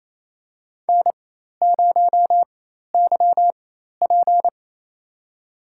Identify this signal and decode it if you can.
Morse code, heard as a keyed tone, about 28 wpm, spaced as for 13 wpm, 700 Hz. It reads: N0YP
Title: USA Call Signs - 4x Character Spacing - 28wpm